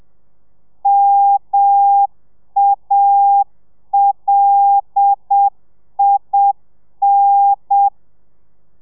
Morse code, heard as a keyed tone, about 7 wpm, 800 Hz. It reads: MALIN
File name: telegrafia / i2tfj / Lez5 30